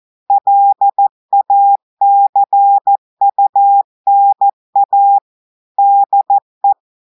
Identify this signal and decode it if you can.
Morse code, heard as a keyed tone, about 14 wpm, 800 Hz. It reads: LACUNA DE
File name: audios / portugais / portugais texte en clair 1 vitesse 14 mots minutes